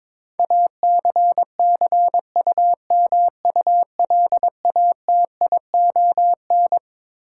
Morse code, heard as a keyed tone, about 22 wpm, 700 Hz. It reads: ACCUMULATION
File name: Twelve Letter Words - 22wpm